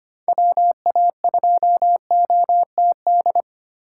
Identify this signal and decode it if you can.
Morse code, heard as a keyed tone, about 25 wpm, 700 Hz. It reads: WA2OTD
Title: USA Call Signs - 4x Character Spacing - 25wpm